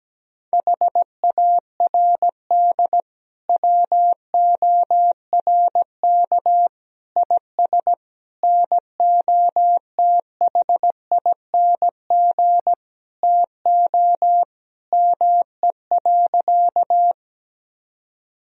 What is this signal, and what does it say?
Morse code, heard as a keyed tone, about 17 wpm, 700 Hz. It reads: HARD WORK IS NOTHING TO ME.
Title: Sentences from Top 1000 Words - Review Part 1 - 17wpm